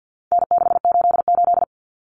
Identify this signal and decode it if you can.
Morse code, heard as a keyed tone, about 50 wpm, 700 Hz. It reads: N688
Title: Common Cut Numbers - 4-Digit 50wpm